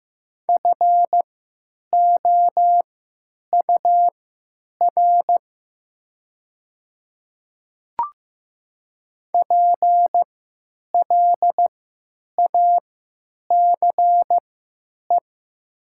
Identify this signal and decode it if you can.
Morse code, heard as a keyed tone, about 15 wpm, spaced as for 9 wpm, 700 Hz. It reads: FOUR PLACE
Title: 3x Character Spacing - Top 300 Words - 15wpm